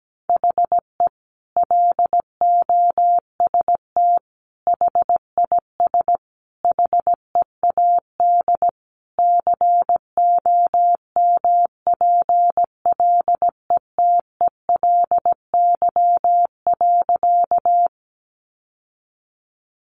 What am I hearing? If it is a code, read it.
Morse code, heard as a keyed tone, about 17 wpm, 700 Hz. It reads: HE LOST HIS HEAD COMPLETELY.
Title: Sentences from Top 1400 Words 17wpm